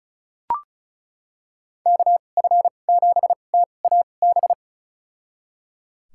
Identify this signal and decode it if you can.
Morse code, heard as a keyed tone, about 34 wpm, spaced as for 25 wpm, 700 Hz. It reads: KF7TAB
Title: USA Call Signs - 6 Characters - 2x Character Spacing - 35wpm